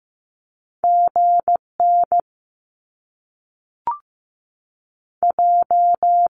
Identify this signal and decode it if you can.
Morse code, heard as a keyed tone, about 15 wpm, 700 Hz. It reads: GN J
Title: Top Two Letter Combinations in Words 15wpm